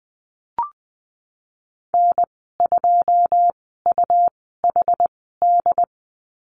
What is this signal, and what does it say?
Morse code, heard as a keyed tone, about 20 wpm, spaced as for 15 wpm, 700 Hz. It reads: N2UHD
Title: 147 Lesson 24g - Letter y - Call Signs 20wpm